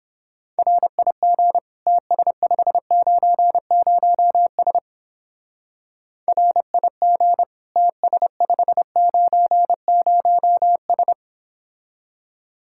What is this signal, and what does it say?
Morse code, heard as a keyed tone, about 29 wpm, 700 Hz. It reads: RIG TS590S RIG TS590S